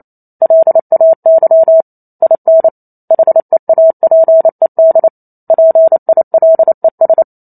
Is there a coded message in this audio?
Yes: Morse code from 0.4 to 7.2 s, heard as a keyed tone, about 28 wpm, 650 Hz. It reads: LAY IN HEAPED PILES